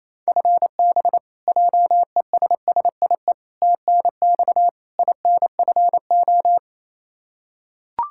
Morse code, heard as a keyed tone, about 28 wpm, 700 Hz. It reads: FB JESSIE TNX INFO
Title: Practice QSO - Exchange 3 - More Info - 28wpm